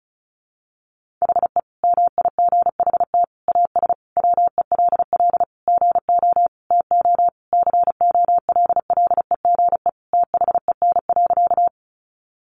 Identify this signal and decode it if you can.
Morse code, heard as a keyed tone, about 34 wpm, 700 Hz. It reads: HE MIGHT AS WELL GO TO COLLEGE THEN.
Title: Sentences from Top 1300 Words 35wpm